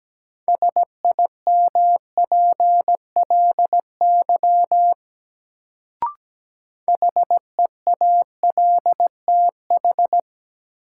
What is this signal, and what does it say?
Morse code, heard as a keyed tone, about 17 wpm, 700 Hz. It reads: SIMPLY HEALTH
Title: Sentences from Top 600 Words - New Word Drill - 17wpm